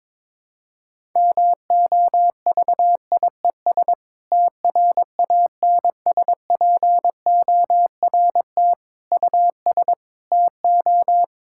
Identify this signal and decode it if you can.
Morse code, heard as a keyed tone, about 21 wpm, 700 Hz. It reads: MOVIES TRANSPORT US TO
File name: Difficult Sentences with Hints - 22wpm